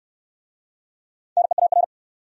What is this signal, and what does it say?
Morse code, heard as a keyed tone, about 68 wpm, 700 Hz. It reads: NEAR